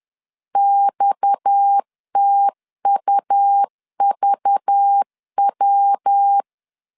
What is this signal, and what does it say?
Morse code, heard as a keyed tone, about 10 wpm, 800 Hz. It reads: XTUVW
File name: LEZIONI / 7 GF9Z / 7 GF9 10wpm 2747